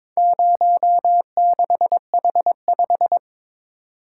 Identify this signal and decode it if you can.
Morse code, heard as a keyed tone, about 22 wpm, 700 Hz. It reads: 06H5
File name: practice / 22wpm / Four Letter-Number Combinations - 22wpm